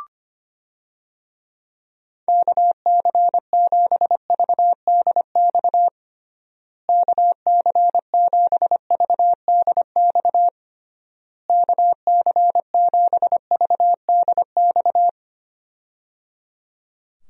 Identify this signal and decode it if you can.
Morse code, heard as a keyed tone, about 25 wpm, 700 Hz. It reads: KC7VDX KC7VDX KC7VDX
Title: USA Call Signs - 6 Characters - Repeated 3x - 25wpm